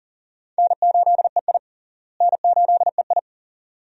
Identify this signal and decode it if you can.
Morse code, heard as a keyed tone, about 37 wpm, 700 Hz. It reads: N8EI N8EI
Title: USA Call Signs - 4 Characters - Repeated 2x - 40wpm